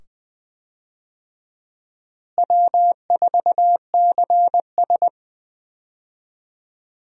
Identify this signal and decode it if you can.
Morse code, heard as a keyed tone, about 20 wpm, 700 Hz. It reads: W4CS
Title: USA Call Signs - 4 Characters - Repeated 2x - 20wpm